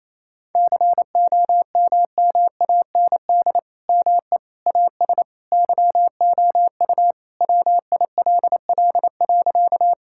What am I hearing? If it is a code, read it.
Morse code, heard as a keyed tone, about 27 wpm, 700 Hz. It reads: COMMAND ME AS YOU WILL.
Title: Sentences from Top 1600 Words 28wpm